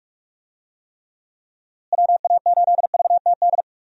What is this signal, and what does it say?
Morse code, heard as a keyed tone, about 44 wpm, 700 Hz. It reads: WA8VTD
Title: USA Call Signs - 6 Characters - Repeated 2x - 45wpm